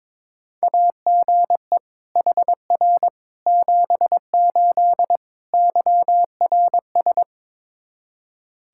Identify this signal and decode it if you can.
Morse code, heard as a keyed tone, about 21 wpm, 700 Hz. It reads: AGE HR 78 YRS
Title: K1USN SST QSO Practice 22wpm